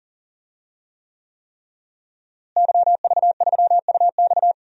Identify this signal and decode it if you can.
Morse code, heard as a keyed tone, about 40 wpm, 700 Hz. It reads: YV3UX
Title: DXCC-100 Call Signs - 5 Characters - Repeated 3x - 40wpm